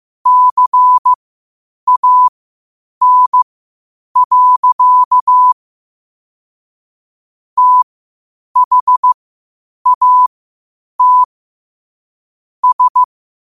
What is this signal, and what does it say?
Morse code, heard as a keyed tone, about 15 wpm, spaced as for 8 wpm, 1000 Hz. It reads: CAN. THAT S
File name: qotdfiles / 8wpm